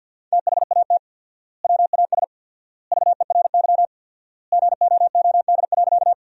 Morse code, heard as a keyed tone, about 50 wpm, 700 Hz. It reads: THAT WAS VERY GOOD.
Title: 4x Word-Spacing - Sentences from Top 200 Words - 50wpm